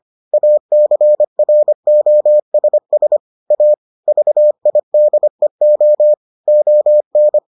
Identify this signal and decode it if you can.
Morse code, heard as a keyed tone, about 25 wpm, 600 Hz. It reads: ACROSS A VIDEO ON